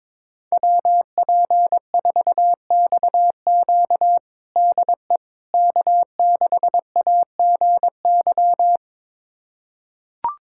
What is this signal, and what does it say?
Morse code, heard as a keyed tone, about 22 wpm, 700 Hz. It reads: WP4XQ DE K6AGY
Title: USA Callsign DE Callsign 22wpm